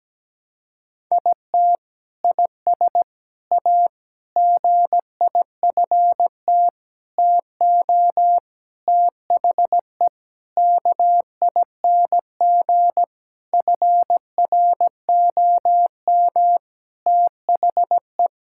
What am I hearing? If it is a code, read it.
Morse code, heard as a keyed tone, about 17 wpm, 700 Hz. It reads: IT IS A GIFT TO THE KING FROM THE
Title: Sentences from Top 1000 Words - Review Part 6 - 17wpm